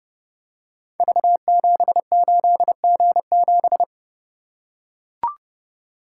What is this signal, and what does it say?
Morse code, heard as a keyed tone, about 29 wpm, 700 Hz. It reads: V78G7
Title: Code Groups - 5 Letter:Numbers - 30wpm